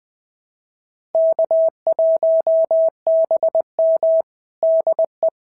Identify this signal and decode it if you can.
Morse code, heard as a keyed tone, about 20 wpm, 690 Hz. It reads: K1BM DE
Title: SOTA QSO Practice 20wpm